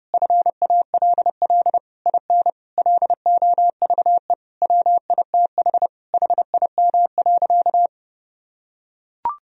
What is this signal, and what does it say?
Morse code, heard as a keyed tone, about 32 wpm, 700 Hz. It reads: FALL IN LOVE WITH HIM.
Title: Sentences from Top 1000 Words - Review Part 4 - 30wpm